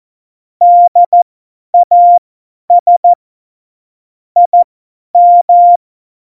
Morse code, heard as a keyed tone, about 14 wpm, spaced as for 10 wpm, 700 Hz. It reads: DAS IM
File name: Morsekurs / Witze / Witz 18. Februar 2020, 50 F70